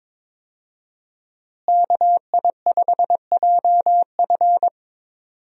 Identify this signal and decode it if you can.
Morse code, heard as a keyed tone, about 22 wpm, 700 Hz. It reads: KI5JF
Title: USA Call Signs - 5 Characters - Repeated 3x - 22wpm